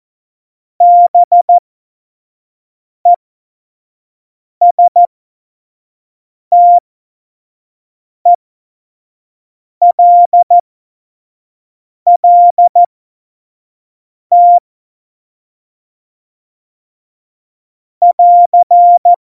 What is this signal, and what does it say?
Morse code, heard as a keyed tone, about 14 wpm, spaced as for 5 wpm, 700 Hz. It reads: BESTELLT <AR>
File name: Morsekurs / Witze / Witz 24. September 2019, 25 F70